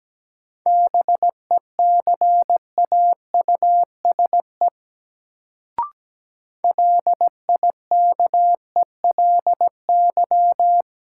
Morse code, heard as a keyed tone, about 17 wpm, 700 Hz. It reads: BECAUSE LIKELY
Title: Sentences from Top 600 Words - New Word Drill - 17wpm